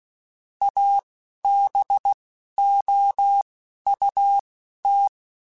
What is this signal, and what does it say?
Morse code, heard as a keyed tone, about 16 wpm, spaced as for 11 wpm, 800 Hz. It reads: ABOUT
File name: (spaced) / QSOs Spaced-mp3s / WD0FBT 1 @15 sp